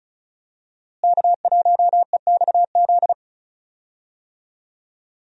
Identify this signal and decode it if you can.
Morse code, heard as a keyed tone, about 36 wpm, 700 Hz. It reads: K1EXZ
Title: USA Call Signs - 5 Characters - Repeated 3x - 35wpm